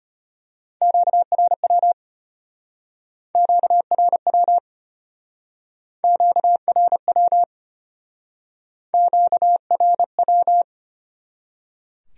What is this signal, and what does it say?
Morse code, heard as a keyed tone, about 32 wpm, 700 Hz. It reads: QRW QRW QRW QRW